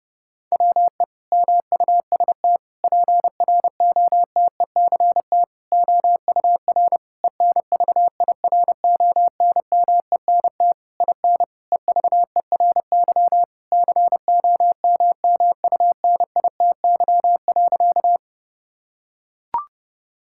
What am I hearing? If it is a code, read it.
Morse code, heard as a keyed tone, about 30 wpm, 700 Hz. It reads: WE MUST PROTECT OUR ENVIRONMENT IN EVERY COMMUNITY.
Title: Sentences from Top 1000 Words - Review Part 5 - 30wpm